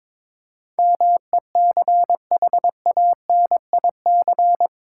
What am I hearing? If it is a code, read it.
Morse code, heard as a keyed tone, about 22 wpm, 700 Hz. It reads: MECHANIC
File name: practice / 22wpm / Eight Letter Words - 22wpm